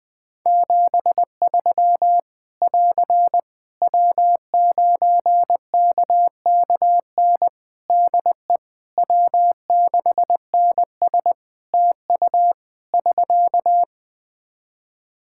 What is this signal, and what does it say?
Morse code, heard as a keyed tone, about 21 wpm, 700 Hz. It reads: 73 <AR> W9KKN DE W6NS TU <SK>
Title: Practice QSO - Exchange 4 - Ending - 20wpm